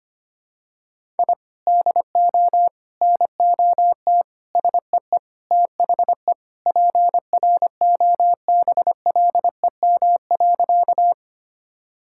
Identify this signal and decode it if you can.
Morse code, heard as a keyed tone, about 25 wpm, 700 Hz. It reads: I DO NOT SEE THE PROBLEM.
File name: practice / 25wpm / Sentences with 4-Word Phrases - 25wpm